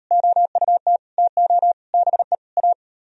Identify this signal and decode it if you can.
Morse code, heard as a keyed tone, about 38 wpm, 700 Hz. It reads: OUT TO BE A